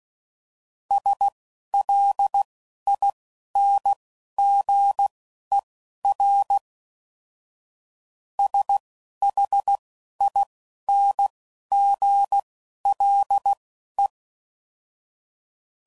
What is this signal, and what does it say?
Morse code, heard as a keyed tone, about 16 wpm, spaced as for 12 wpm, 800 Hz. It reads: SLINGER SHINGLE